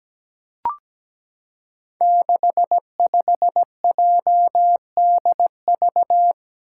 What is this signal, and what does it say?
Morse code, heard as a keyed tone, about 17 wpm, 700 Hz. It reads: 65JDV
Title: Five Challenging Letter-Number Combinations 17wpm